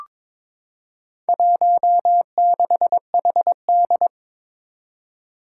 Morse code, heard as a keyed tone, about 22 wpm, 700 Hz. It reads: 16HD
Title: Four Challenging Letter-Number Combinations 22wpm